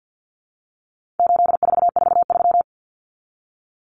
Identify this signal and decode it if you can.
Morse code, heard as a keyed tone, about 50 wpm, 700 Hz. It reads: 8443
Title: Common Cut Numbers - 4-Digit 50wpm